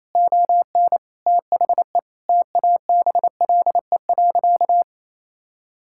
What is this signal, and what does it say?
Morse code, heard as a keyed tone, about 27 wpm, 700 Hz. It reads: ON THE TABLE.
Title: Sentences from Top 1600 Words 28wpm